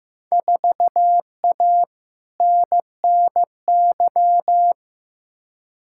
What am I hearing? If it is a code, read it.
Morse code, heard as a keyed tone, about 15 wpm, 700 Hz. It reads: 4A NNY
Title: Field Day Exchanges - 15wpm